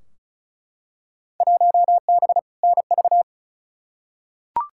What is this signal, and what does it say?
Morse code, heard as a keyed tone, about 36 wpm, 700 Hz. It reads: 1B NV
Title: Field Day Exchanges - 35wpm